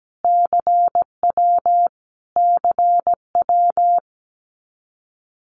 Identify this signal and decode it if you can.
Morse code, heard as a keyed tone, about 17 wpm, 700 Hz. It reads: CW CW
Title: Abbreviations Repeated 3x - 17wpm